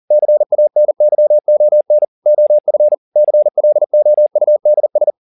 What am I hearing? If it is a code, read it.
Morse code, heard as a keyed tone, about 40 wpm, 600 Hz. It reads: CANYON OF CLOUDS
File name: the old man and the sea 2 40wpm